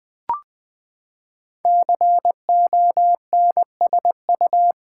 Morse code, heard as a keyed tone, about 20 wpm, 700 Hz. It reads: CONSU
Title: Build A Word - 5 Letters At A Time - 10-Letter Words - 20wpm